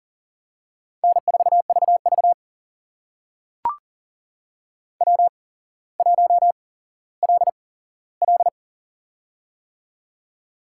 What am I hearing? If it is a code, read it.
Morse code, heard as a keyed tone, about 40 wpm, spaced as for 17 wpm, 700 Hz. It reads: N4VV W1LL
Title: USA Call Signs - 4 Characters - 8x Character Spacing - 40wpm